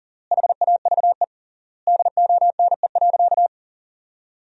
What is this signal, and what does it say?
Morse code, heard as a keyed tone, about 40 wpm, 700 Hz. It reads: HAVE DONE.
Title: 3x Word-Spacing - Sentences from Top 500 Words - 40wpm